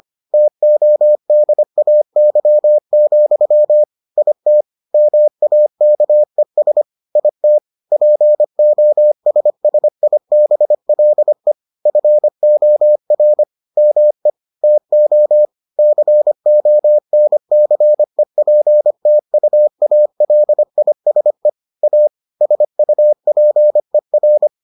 Morse code, heard as a keyed tone, about 25 wpm, 600 Hz. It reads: TODAY, IT MAKES IT POSSIBLE FOR ME TO CONCEPTUALISE A SUPER